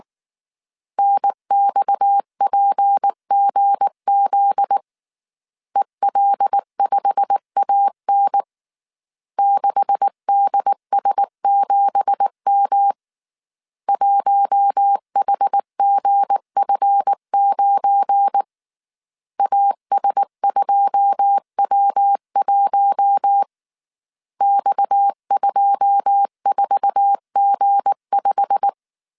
Morse code, heard as a keyed tone, about 19 wpm, 800 Hz. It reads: NXPGZ EL5AN 6DS7M 1HGF9 AS2W1 X24G5
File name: CW / DET / 1 9 1500C S A 18wpm 2459